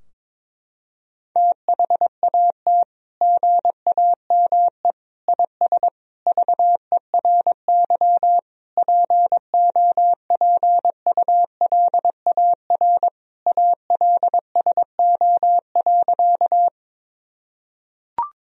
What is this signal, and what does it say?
Morse code, heard as a keyed tone, about 22 wpm, 700 Hz. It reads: THAT GAME IS VERY POPULAR ALSO.
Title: Sentences from Top 1000 Words - Review Part 3 - 22wpm